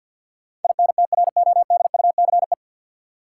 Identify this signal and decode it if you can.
Morse code, heard as a keyed tone, about 53 wpm, 700 Hz. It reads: INTRODUCE
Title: Sentences from Top 1600 Words - New Word Drill - 50wpm